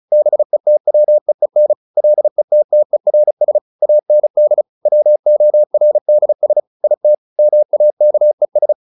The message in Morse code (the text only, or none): BETWEEN LETTERS AND WORDS IT MAKES